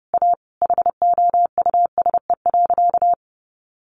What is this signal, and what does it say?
Morse code, heard as a keyed tone, about 30 wpm, 700 Hz. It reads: A HOUSE.
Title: Sentences from Top 600 Words 30wpm